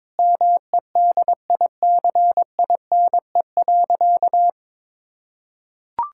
Morse code, heard as a keyed tone, about 22 wpm, 700 Hz. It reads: MEDICINE.
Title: Sentences from Top 1000 Words - Review Part 6 - 22wpm